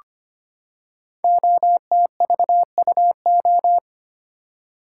Morse code, heard as a keyed tone, about 25 wpm, 700 Hz. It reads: OTVUO
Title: Code Groups - 5 Letter:Numbers - 25wpm